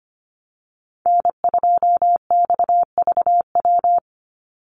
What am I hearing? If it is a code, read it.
Morse code, heard as a keyed tone, about 25 wpm, 700 Hz. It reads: N2XVW
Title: USA Call Signs 25wpm